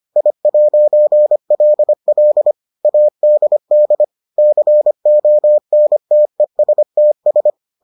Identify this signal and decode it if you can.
Morse code, heard as a keyed tone, about 25 wpm, 600 Hz. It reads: I'LL ADD CONTESTS